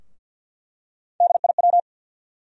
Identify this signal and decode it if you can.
Morse code, heard as a keyed tone, about 50 wpm, 700 Hz. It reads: DIW
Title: USA Call Sign Postfixes - 3 Letters 50wpm